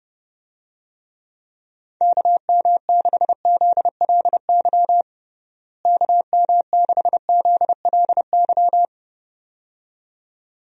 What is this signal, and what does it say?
Morse code, heard as a keyed tone, about 30 wpm, 700 Hz. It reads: KM6ZLY KM6ZLY
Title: USA Call Signs - 6 Characters - Repeated 2x - 30wpm